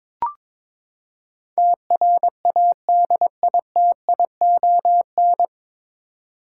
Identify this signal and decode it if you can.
Morse code, heard as a keyed tone, about 22 wpm, 700 Hz. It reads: TRADITION
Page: Morse Code Ninja practice 22wpm